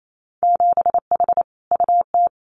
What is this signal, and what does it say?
Morse code, heard as a keyed tone, about 28 wpm, 700 Hz. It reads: 7H UT
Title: Winter Field Day Exchanges Encore 28wpm